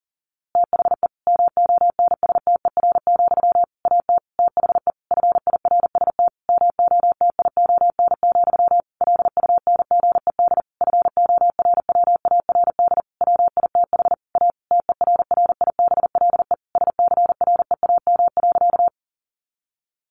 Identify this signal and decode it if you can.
Morse code, heard as a keyed tone, about 40 wpm, 700 Hz. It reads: THE MONSTER, AT THE FIRST MOTION, LUNGED FORWARD WITH A TERRIBLE SCREAM.
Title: A Sound Of Thunder 40wpm